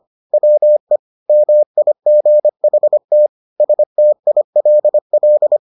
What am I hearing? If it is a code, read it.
Morse code, heard as a keyed tone, about 25 wpm, 600 Hz. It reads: WE MIGHT STILL